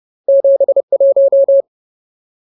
Morse code, heard as a keyed tone, about 30 wpm, 550 Hz. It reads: 71
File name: Ultimate Word List / 30WPM Ultimate